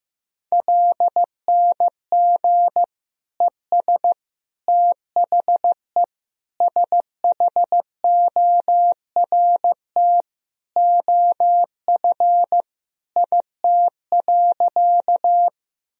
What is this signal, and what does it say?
Morse code, heard as a keyed tone, about 15 wpm, 700 Hz. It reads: LNG ES THE SHORT OF IT.
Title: Sentences from Top 400 Words and Abbreviations 15wpm